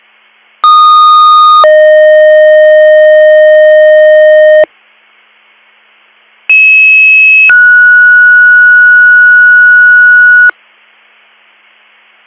Two-tone page, heard as a one-second tone, then a three-second tone, about 1230 then 620 Hz, about 2600 then 1460 Hz. It is heard from 0.6 to 10.5 s.